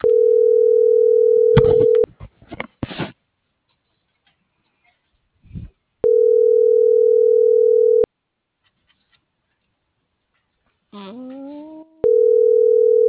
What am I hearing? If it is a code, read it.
Ringback tone (North America), 440 + 480 Hz.
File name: CENTRAL MULTIMODAL / out viridiana.juarez - .80486